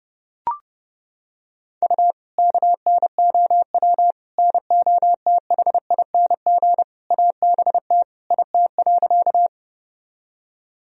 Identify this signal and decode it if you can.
Morse code, heard as a keyed tone, about 30 wpm, 700 Hz. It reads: U KNOW NOTHING ABT IT.
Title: Sentences from Top 500 Words and Abbreviations 30wpm